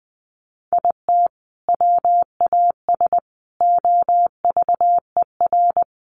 Morse code, heard as a keyed tone, about 20 wpm, 700 Hz. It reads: IT WAS OVER
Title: Sentences from Top 600 Words 20wpm